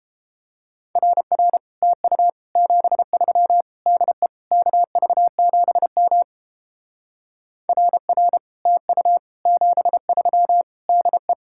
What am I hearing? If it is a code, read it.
Morse code, heard as a keyed tone, about 32 wpm, 700 Hz. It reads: RR TU 73 DE KV7M RR TU 73 DE